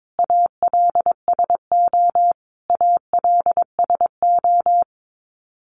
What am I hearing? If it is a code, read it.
Morse code, heard as a keyed tone, about 22 wpm, 700 Hz. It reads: ALSO ALSO
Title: Top 500 Words Repeated 22wpm